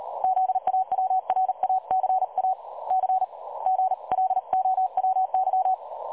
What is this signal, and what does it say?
Morse code, heard as a keyed tone, about 38 wpm, spaced as for 33 wpm, 750 Hz. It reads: ZAWRACAC GLOWY
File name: dlazaawansowanych / nagraniazpasma / websdr recording 3565.0kHz